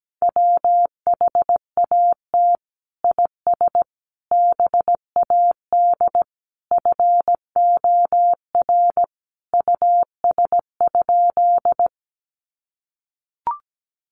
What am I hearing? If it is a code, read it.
Morse code, heard as a keyed tone, about 17 wpm, 700 Hz. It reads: WHAT IS BAD FOR US?
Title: Sentences from Top 700 Words 17wpm